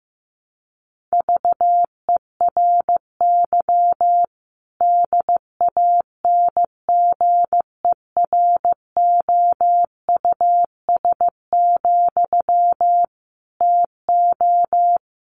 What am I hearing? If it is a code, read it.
Morse code, heard as a keyed tone, about 15 wpm, 700 Hz. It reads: VERY DANGEROUS, TO
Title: Sentences from Top 1200 Words 15wpm